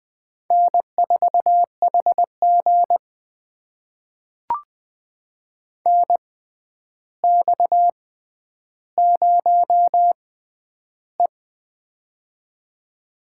Morse code, heard as a keyed tone, about 20 wpm, spaced as for 10 wpm, 700 Hz. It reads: N4HG NX0E
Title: USA Call Signs - 4 Characters - 6x Character Spacing - 20wpm